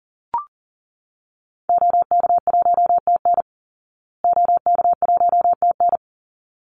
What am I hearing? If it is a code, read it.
Morse code, heard as a keyed tone, about 40 wpm, 700 Hz. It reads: OK1TN OK1TN
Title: Call Signs Repeated 3x - 40wpm